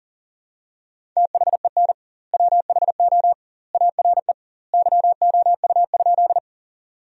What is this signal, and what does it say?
Morse code, heard as a keyed tone, about 40 wpm, 700 Hz. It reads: THEN WHO ARE YOU?
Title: 2x Word-Spacing - Sentences from Top 100 Words - 40wpm